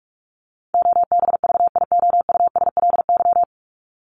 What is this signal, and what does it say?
Morse code, heard as a keyed tone, about 44 wpm, 700 Hz. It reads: OBVIOUSLY
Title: Top 1501-1750 Words 45wpm